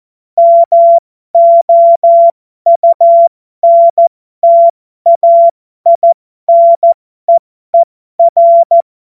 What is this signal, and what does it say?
Morse code, heard as a keyed tone, about 14 wpm, spaced as for 12 wpm, 680 Hz. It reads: MOUNTAINEER